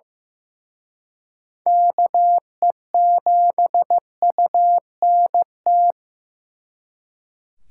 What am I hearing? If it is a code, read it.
Morse code, heard as a keyed tone, about 18 wpm, 700 Hz. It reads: KE7UNT
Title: Speed Racing - USA 2x3 Call Signs 15wpm